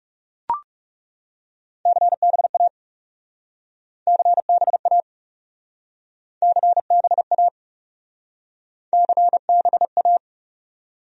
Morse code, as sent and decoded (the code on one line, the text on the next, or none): -.-. -... .- / -.-. -... .- / -.-. -... .- / -.-. -... .-
CBA CBA CBA CBA